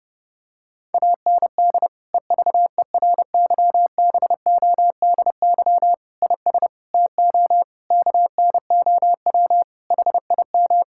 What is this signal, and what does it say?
Morse code, heard as a keyed tone, about 30 wpm, 700 Hz. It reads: AND EVERYBODY IS TO KNOW HIM
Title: Sentences from Top 1600 Words 30wpm